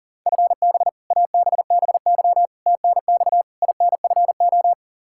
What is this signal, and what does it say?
Morse code, heard as a keyed tone, about 40 wpm, 700 Hz. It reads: FB ABBY TNX INFO
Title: Practice QSO - Exchange 3 - More Info - 40wpm